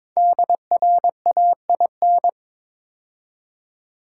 Morse code, heard as a keyed tone, about 23 wpm, 700 Hz. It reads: DRAIN
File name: 5 Letter Words - First 3 Letters Given - 22wpm